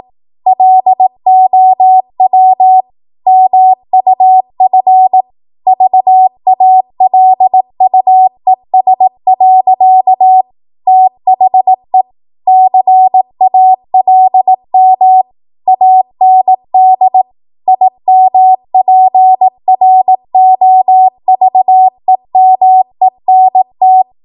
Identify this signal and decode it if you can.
Morse code, heard as a keyed tone, about 18 wpm, 750 Hz. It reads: LOW MUF VALUES. THE CALM AND IMPROVEMENT
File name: Morse / 18WPM